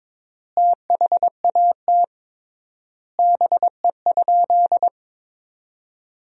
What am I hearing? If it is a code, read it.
Morse code, heard as a keyed tone, about 22 wpm, 700 Hz. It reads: THAT BE?
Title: 3x Word-Spacing - Sentences from Top 100 Words - 22wpm